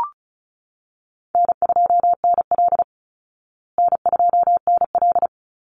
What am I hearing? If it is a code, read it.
Morse code, heard as a keyed tone, about 34 wpm, 700 Hz. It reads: N2NL N2NL
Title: Call Signs Repeated 2x - 35wpm